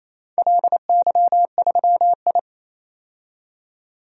Morse code, heard as a keyed tone, about 28 wpm, 700 Hz. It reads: LY3I
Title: DXCC-100 Call Signs - 4 Characters - 28wpm